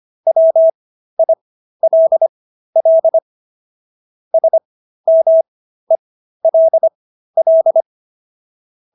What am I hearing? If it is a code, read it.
Morse code, heard as a keyed tone, about 25 wpm, spaced as for 13 wpm, 650 Hz. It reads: WILL SMELL